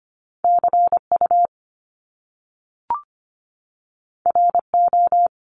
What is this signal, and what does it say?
Morse code, heard as a keyed tone, about 25 wpm, 700 Hz. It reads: CU RO
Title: Top Two Letter Combinations in Words 25wpm